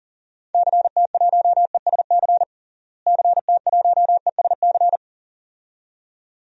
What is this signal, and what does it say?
Morse code, heard as a keyed tone, about 40 wpm, 700 Hz. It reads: CT1ESC CT1ESC
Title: DXCC-100 Call Signs - 6 Characters - Repeated 2x - 40wpm